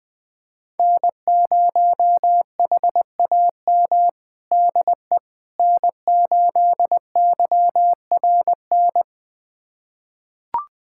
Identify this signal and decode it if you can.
Morse code, heard as a keyed tone, about 20 wpm, 700 Hz. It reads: N0HAM DE N8YRN
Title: USA Callsign DE Callsign - 5 Characters - 20wpm